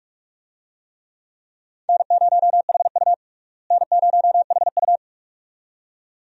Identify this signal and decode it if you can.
Morse code, heard as a keyed tone, about 44 wpm, 700 Hz. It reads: N0HU N0HU
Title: USA Call Signs - 4 Characters - Repeated 2x - 45wpm